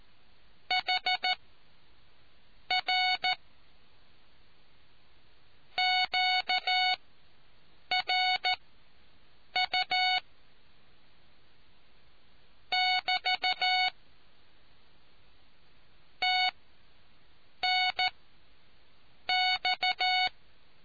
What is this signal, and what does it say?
Morse code, heard as a keyed tone, about 14 wpm, spaced as for 6 wpm, 740 Hz. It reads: HR QRU <BT> TNX